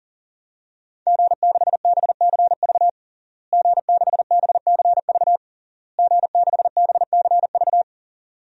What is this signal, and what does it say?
Morse code, heard as a keyed tone, about 40 wpm, 700 Hz. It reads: G6BCV G6BCV G6BCV
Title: DXCC-100 Call Signs - 5 Characters - Repeated 3x - 40wpm